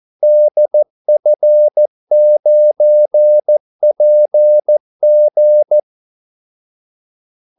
Morse code, heard as a keyed tone, about 14 wpm, 600 Hz. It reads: DF9PG
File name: Callsigns-14wpm